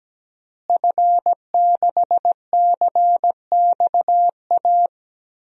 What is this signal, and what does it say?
Morse code, heard as a keyed tone, about 17 wpm, 700 Hz. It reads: F6CXA
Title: DXCC-100 Call Signs - 5 Characters - Repeated 3x - 17wpm